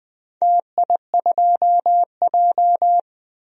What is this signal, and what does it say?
Morse code, heard as a keyed tone, about 20 wpm, 700 Hz. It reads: TI2J
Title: DXCC-100 Call Signs - 4 Characters - 20wpm